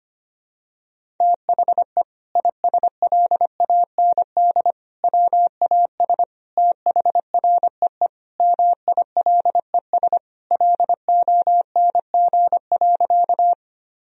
Morse code, heard as a keyed tone, about 25 wpm, 700 Hz. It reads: THE ISLAND WAS THREE MILES LONG.
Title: Sentences from Top 1000 Words - Review Part 3 - 25wpm